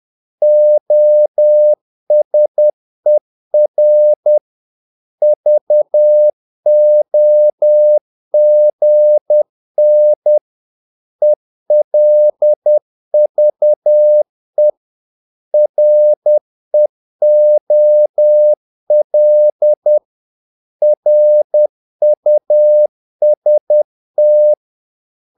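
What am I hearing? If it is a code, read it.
Morse code, heard as a keyed tone, about 10 wpm, 600 Hz. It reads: OSER VOGN ELVE REOL RUST
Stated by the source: Morse practice of short Danish words at 10 wpm, from CW med Gnister.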